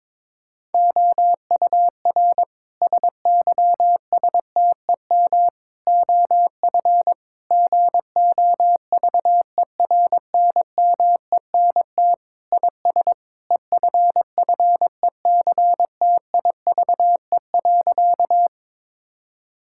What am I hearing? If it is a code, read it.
Morse code, heard as a keyed tone, about 22 wpm, 700 Hz. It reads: OUR SYSTEM OF GOVERNMENT IS EFFECTIVE.
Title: Sentences from Top 1000 Words - Review Part 6 - 22wpm